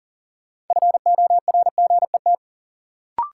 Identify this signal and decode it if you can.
Morse code, heard as a keyed tone, about 40 wpm, 700 Hz. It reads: FORGET
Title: Sentences from Top 1200 Words - New Word Drill Alt - 40wpm